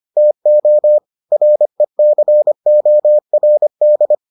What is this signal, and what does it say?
Morse code, heard as a keyed tone, about 25 wpm, 600 Hz. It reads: TO RECORD